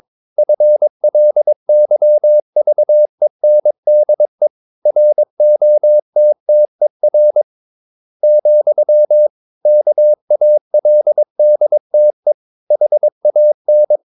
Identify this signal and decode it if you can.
Morse code, heard as a keyed tone, about 22 wpm, 600 Hz. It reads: FLYVENDE ROTTER, KALDTE HAN
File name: Ferie QRS 22wpm